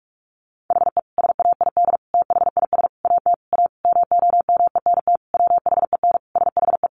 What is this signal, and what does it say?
Morse code, heard as a keyed tone, about 44 wpm, 700 Hz. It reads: HE SAID THIS AT A MOMENT WHEN SHE
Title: Sentences from Top 1000 Words 45wpm